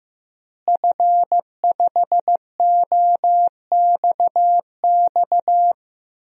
Morse code, heard as a keyed tone, about 15 wpm, 700 Hz. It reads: F5OXX
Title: DXCC-100 Call Signs - 5 Characters - Repeated 2x - 15wpm